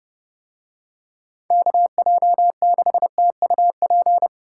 Morse code, heard as a keyed tone, about 28 wpm, 700 Hz. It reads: KJ6TUP